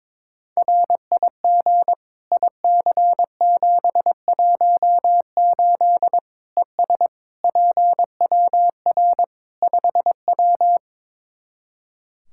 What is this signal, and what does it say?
Morse code, heard as a keyed tone, about 21 wpm, 700 Hz. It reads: RIG IC718 ES PWR 5W
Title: Practice QSO - Exchange 2 - Info - 22wpm